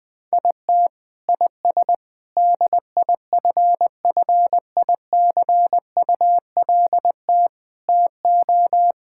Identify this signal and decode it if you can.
Morse code, heard as a keyed tone, about 20 wpm, 700 Hz. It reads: IT IS DIFFICULT TO
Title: Sentences from Top 1000 Words - Review Part 5 - 20wpm